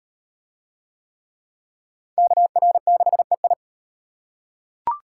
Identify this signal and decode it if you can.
Morse code, heard as a keyed tone, about 35 wpm, 700 Hz. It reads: KR6EI